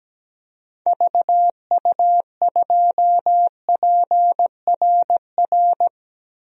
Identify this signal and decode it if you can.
Morse code, heard as a keyed tone, about 17 wpm, 700 Hz. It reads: VU2PRR
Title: DXCC-100 Call Signs - 6 Characters - 17wpm